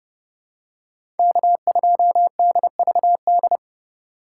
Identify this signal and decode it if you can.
Morse code, heard as a keyed tone, about 30 wpm, 700 Hz. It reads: K2DVD
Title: USA Call Signs - 5 Characters - 2x Character Spacing - 30wpm